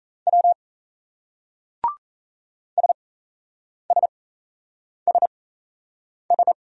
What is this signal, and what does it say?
Morse code, heard as a keyed tone, about 37 wpm, spaced as for 8 wpm, 700 Hz. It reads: W SSSS